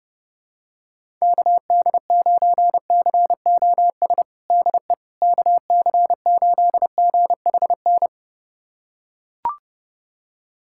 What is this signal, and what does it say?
Morse code, heard as a keyed tone, about 30 wpm, 700 Hz. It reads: KD9COS DE KC8GHN
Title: USA Callsign DE Callsign - 6 Characters - 30wpm